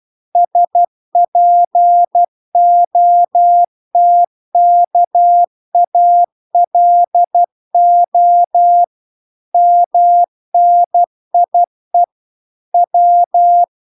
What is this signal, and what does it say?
Morse code, heard as a keyed tone, about 12 wpm, 700 Hz. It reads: SPOTKALO MNIE W